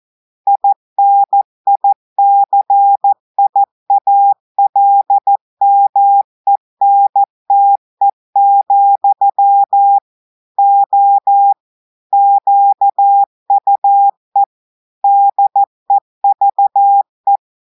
Morse code, heard as a keyed tone, about 14 wpm, 800 Hz. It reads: INICIALMENTE, O QUE DEVE
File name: audios / portugais / portugais texte en clair 2 vitesse 14 mots minutes